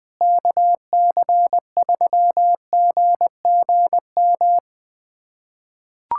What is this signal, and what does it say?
Morse code, heard as a keyed tone, about 20 wpm, 700 Hz. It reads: KC3GGM
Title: USA Callsign DE Callsign - 6 Characters - 20wpm